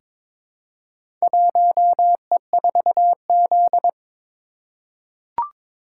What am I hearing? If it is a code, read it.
Morse code, heard as a keyed tone, about 21 wpm, 700 Hz. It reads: 1E4Z
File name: practice / 22wpm / Four Letter-Number Combinations - 22wpm